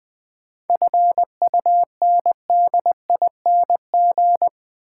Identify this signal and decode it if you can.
Morse code, heard as a keyed tone, about 20 wpm, 690 Hz. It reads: FUNDING
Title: 7 Letter Words - First 3 Letters Given - 20wpm